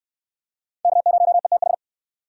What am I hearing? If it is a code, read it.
Morse code, heard as a keyed tone, about 63 wpm, 700 Hz. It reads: K0ETH